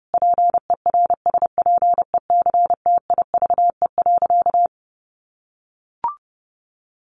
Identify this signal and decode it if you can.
Morse code, heard as a keyed tone, about 28 wpm, 700 Hz. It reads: PERSPECTIVE.
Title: Sentences from Top 1200 Words 30wpm